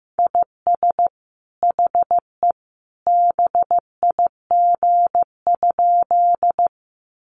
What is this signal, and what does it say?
Morse code, heard as a keyed tone, about 15 wpm, 700 Hz. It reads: IS HE BIG?
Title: Sentences from Top 200 Words 15wpm